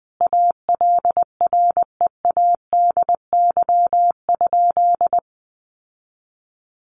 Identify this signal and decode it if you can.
Morse code, heard as a keyed tone, about 20 wpm, 700 Hz. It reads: ALREADY?
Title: Sentences from Top 700 Words 20wpm